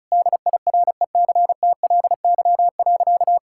Morse code, heard as a keyed tone, about 34 wpm, 700 Hz. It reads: DIRECTLY.
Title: Sentences from Top 1000 Words - Review Part 5 - 35wpm